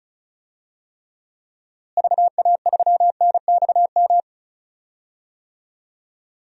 Morse code, heard as a keyed tone, about 36 wpm, 700 Hz. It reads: VA3NXM
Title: DXCC-100 Call Signs - 6 Characters - Repeated 2x - 35wpm